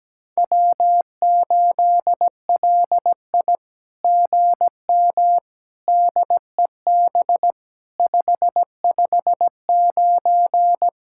Morse code, heard as a keyed tone, about 17 wpm, 700 Hz. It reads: W8LI GM DEB 559
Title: SOTA QSO Practice 17wpm